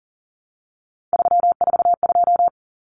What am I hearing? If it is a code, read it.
Morse code, heard as a keyed tone, about 40 wpm, 700 Hz. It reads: 342
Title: Numbers - 3-Digit 40wpm